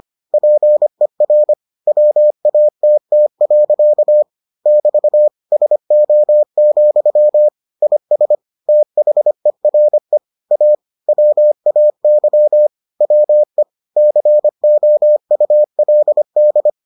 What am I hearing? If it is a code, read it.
Morse code, heard as a keyed tone, about 25 wpm, 600 Hz. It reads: PER WATT. <BT> SO, IS THERE A WAY WE COULD